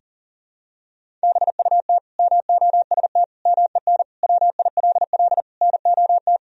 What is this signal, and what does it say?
Morse code, heard as a keyed tone, about 40 wpm, 700 Hz. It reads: BUT MOST MEN WILL NOT
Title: Sentences from Top 1000 Words - Review Part 1 - 40wpm